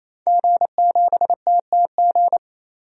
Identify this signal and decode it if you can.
Morse code, heard as a keyed tone, about 27 wpm, 700 Hz. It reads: G7TTG
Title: DXCC-100 Call Signs - 5 Characters - 28wpm